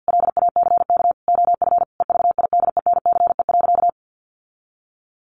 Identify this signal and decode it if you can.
Morse code, heard as a keyed tone, about 50 wpm, 700 Hz. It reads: LACK OF EVIDENCE.
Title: Sentences from Top 1300 Words 50wpm